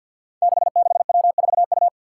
Morse code, heard as a keyed tone, about 51 wpm, 700 Hz. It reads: 66W4U
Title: Five Challenging Letter-Number Combinations 50wpm